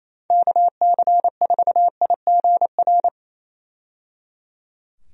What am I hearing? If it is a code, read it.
Morse code, heard as a keyed tone, about 28 wpm, 700 Hz. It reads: KC4IGR
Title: USA Call Signs - 6 Characters - 28wpm